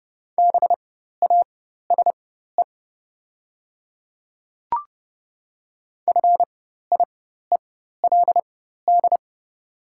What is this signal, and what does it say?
Morse code, heard as a keyed tone, about 30 wpm, spaced as for 14 wpm, 690 Hz. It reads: BASE FIELD
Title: 4x Character Spacing - Top 500 Words - 30wpm